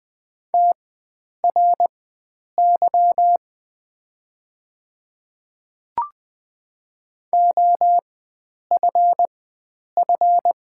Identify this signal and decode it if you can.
Morse code, heard as a keyed tone, about 20 wpm, spaced as for 9 wpm, 700 Hz. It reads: TRY OFF
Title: 4x Character Spacing - Top 500 Words - 20wpm